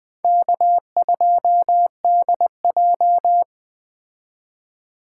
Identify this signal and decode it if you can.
Morse code, heard as a keyed tone, about 20 wpm, 700 Hz. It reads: K2DJ